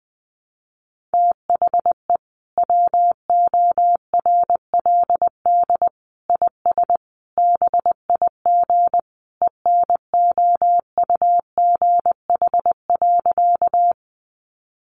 Morse code, heard as a keyed tone, about 19 wpm, 700 Hz. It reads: THE WORLD IS BIG ENOUGH.
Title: Sentences from Top 400 Words 20wpm